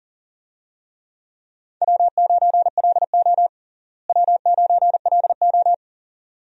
Morse code, heard as a keyed tone, about 40 wpm, 700 Hz. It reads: W9LO W9LO
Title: USA Call Signs - 4 Characters - Repeated 2x - 40wpm